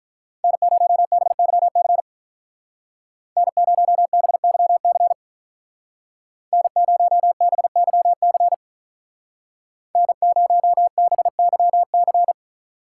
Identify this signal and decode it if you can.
Morse code, heard as a keyed tone, about 47 wpm, 700 Hz. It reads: N0BYC N0BYC N0BYC N0BYC